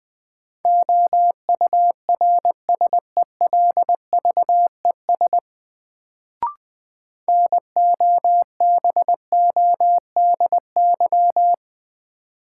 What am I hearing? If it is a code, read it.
Morse code, heard as a keyed tone, about 20 wpm, 700 Hz. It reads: OURSELVES NOBODY
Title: Sentences from Top 1600 Words - New Word Drill Alt - 20wpm